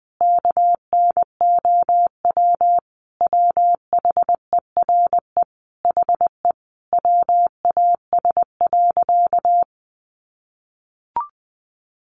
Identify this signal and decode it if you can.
Morse code, heard as a keyed tone, about 20 wpm, 700 Hz. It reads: KNOW WHERE HE WAS.
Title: Sentences from Top 600 Words 20wpm